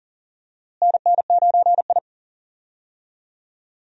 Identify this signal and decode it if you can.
Morse code, heard as a keyed tone, about 40 wpm, 700 Hz. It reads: NN9I
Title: USA Call Signs - 4 Characters - Repeated 2x - 40wpm